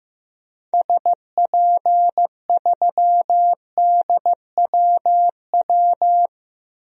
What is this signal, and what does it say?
Morse code, heard as a keyed tone, about 15 wpm, 700 Hz. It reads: SP3DWW
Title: DXCC-100 Call Signs - 6 Characters - 15wpm